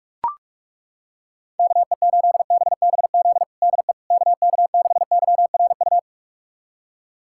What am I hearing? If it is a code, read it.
Morse code, heard as a keyed tone, about 41 wpm, 700 Hz. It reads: KE8BBZ DE KK6YRU
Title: USA Callsign DE Callsign - 6 Characters - 45wpm